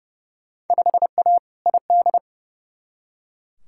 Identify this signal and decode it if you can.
Morse code, heard as a keyed tone, about 29 wpm, 700 Hz. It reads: 5A ID
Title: Field Day Exchanges Encore - 30wpm